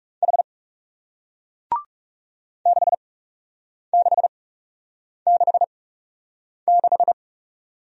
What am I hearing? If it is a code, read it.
Morse code, heard as a keyed tone, about 42 wpm, spaced as for 8 wpm, 700 Hz. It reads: H 6666